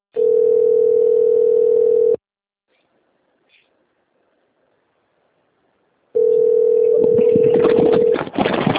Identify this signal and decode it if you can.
Ringback tone (North America), 440 + 480 Hz.
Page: Seu Credito